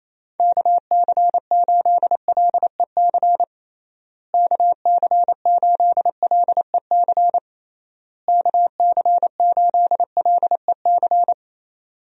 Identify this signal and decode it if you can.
Morse code, heard as a keyed tone, about 27 wpm, 700 Hz. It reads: KC8LEC KC8LEC KC8LEC
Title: USA Call Signs - 6 Characters - Repeated 3x - 28wpm